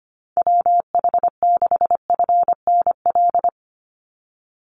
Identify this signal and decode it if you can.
Morse code, heard as a keyed tone, about 25 wpm, 700 Hz. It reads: WH6FNL
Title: USA Call Signs 25wpm